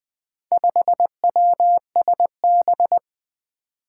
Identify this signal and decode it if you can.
Morse code, heard as a keyed tone, about 20 wpm, 700 Hz. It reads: 5WSB
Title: Four Challenging Letter-Number Combinations 20wpm